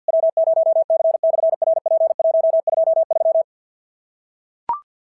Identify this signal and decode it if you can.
Morse code, heard as a keyed tone, about 46 wpm, 660 Hz. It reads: W0C/RP123
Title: SOTA QSO Practice 50wpm